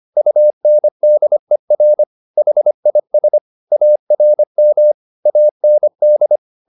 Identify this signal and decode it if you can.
Morse code, heard as a keyed tone, about 25 wpm, 600 Hz. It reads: UNDER HIS ARM AND